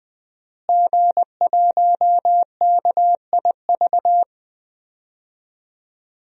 Morse code, heard as a keyed tone, about 20 wpm, 700 Hz. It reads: G1KIV
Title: DXCC-100 Call Signs - 5 Characters - Repeated 2x - 20wpm